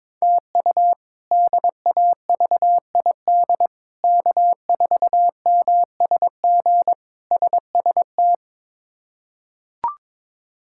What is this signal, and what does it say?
Morse code, heard as a keyed tone, about 22 wpm, 700 Hz. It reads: TU DAVID K4MSG SST
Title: K1USN SST QSO Practice - 22wpm